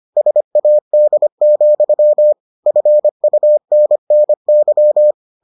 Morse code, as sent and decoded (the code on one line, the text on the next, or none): ... .- -.. --..-- / ..-. ..- -. -. -.--
SAD, FUNNY